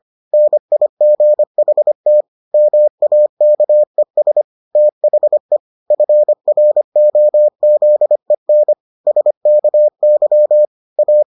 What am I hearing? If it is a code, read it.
Morse code, heard as a keyed tone, about 25 wpm, 600 Hz. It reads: NIGHT MAKES THE FROZEN SKY A